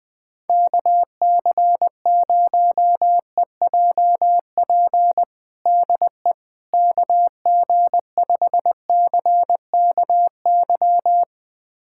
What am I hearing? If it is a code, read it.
Morse code, heard as a keyed tone, about 21 wpm, 700 Hz. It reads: KC0EJP DE KG5CKY
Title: USA Callsign DE Callsign - 6 Characters - 20wpm